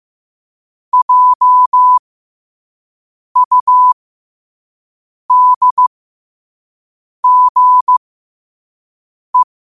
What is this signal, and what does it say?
Morse code, heard as a keyed tone, about 15 wpm, spaced as for 5 wpm, 1000 Hz. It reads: JUDGE
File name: qotdfiles / 5wpm